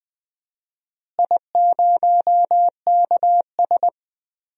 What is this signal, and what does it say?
Morse code, heard as a keyed tone, about 20 wpm, 700 Hz. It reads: I0KS
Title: DXCC-100 Call Signs - 4 Characters - Repeated 2x - 20wpm